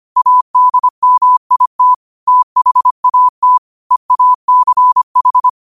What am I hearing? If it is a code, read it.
Morse code, heard as a keyed tone, about 25 wpm, 1000 Hz. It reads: ADMIT THAT EACH
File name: qotdfiles / 25wpm